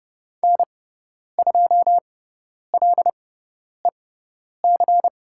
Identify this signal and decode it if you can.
Morse code, heard as a keyed tone, about 30 wpm, spaced as for 10 wpm, 700 Hz. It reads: N2LEC
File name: Lesson 16g - Letter c - Call Signs 30wpm